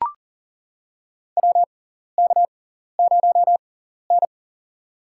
Morse code, heard as a keyed tone, about 39 wpm, spaced as for 14 wpm, 700 Hz. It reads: WK0N
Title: USA Call Signs - 6x Character Spacing - 40wpm